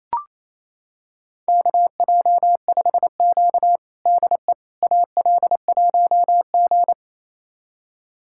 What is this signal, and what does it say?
Morse code, heard as a keyed tone, about 27 wpm, 700 Hz. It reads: KJ5Q DE AL1G
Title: USA Callsign DE Callsign - 4 Characters - 28wpm